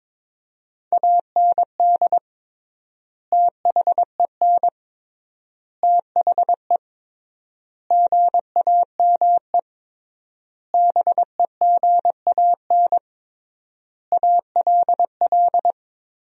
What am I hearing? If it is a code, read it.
Morse code, heard as a keyed tone, about 22 wpm, 700 Hz. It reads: AND THEN THE GAME BEGAN ALL
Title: 3x Word-Spacing - Sentences from Top 500 Words - 22wpm